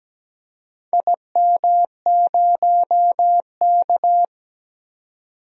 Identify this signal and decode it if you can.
Morse code, heard as a keyed tone, about 17 wpm, 700 Hz. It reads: IM0K
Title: DXCC-100 Call Signs - 4 Characters - 17wpm